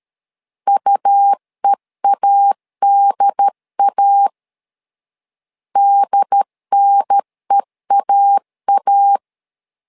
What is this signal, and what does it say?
Morse code, heard as a keyed tone, about 13 wpm, 800 Hz. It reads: UEADA DNEAA
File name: LEZIONI / 2 UHDN / 2 UHDN 12wpm 1456